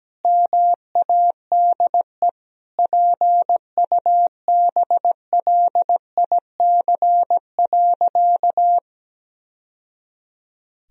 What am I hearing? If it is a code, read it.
Morse code, heard as a keyed tone, about 17 wpm, 700 Hz. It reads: MADE PUBLIC.
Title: Sentences from Top 1000 Words - Review Part 3 - 17wpm